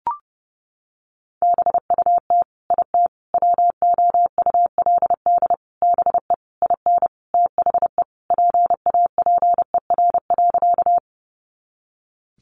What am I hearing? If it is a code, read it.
Morse code, heard as a keyed tone, about 30 wpm, 700 Hz. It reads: BUT IT WOULD BE IN THE PAPER.
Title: Sentences from Top 300 Words 30wpm